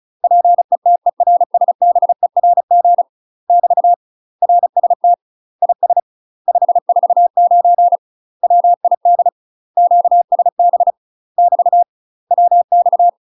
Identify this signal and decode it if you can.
Morse code, heard as a keyed tone, about 34 wpm, 700 Hz. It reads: PETERSBERG <BT> RST IS 549 WID QSB <BT> WX
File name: UK MASTER QSOs TRAINING / 35WPM / UK MASTER QSOs 0004